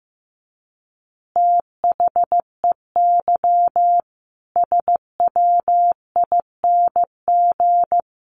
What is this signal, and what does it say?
Morse code, heard as a keyed tone, about 15 wpm, 700 Hz. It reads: THEY SWING
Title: Easy Phrases 15wpm